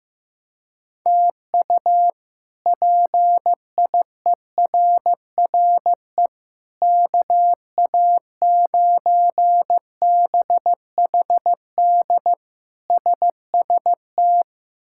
Morse code, heard as a keyed tone, about 15 wpm, 700 Hz. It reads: TU PIERRE KA9BHD SST
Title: K1USN SST QSO Practice - 15wpm